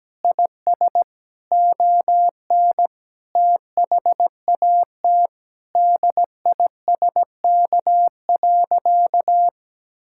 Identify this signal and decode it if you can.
Morse code, heard as a keyed tone, about 17 wpm, 700 Hz. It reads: IS ON THAT DISK.
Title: Sentences from Top 1000 Words - Review Part 6 - 17wpm